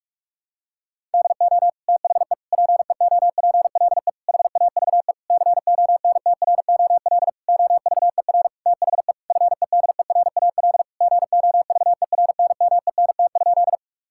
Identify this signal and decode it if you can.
Morse code, heard as a keyed tone, about 46 wpm, 700 Hz. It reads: DO THE PEOPLE HAVE CONTROL OVER THE FEDERAL GOVERNMENT?
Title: Sentences from Top 1000 Words - Review Part 5 - 45wpm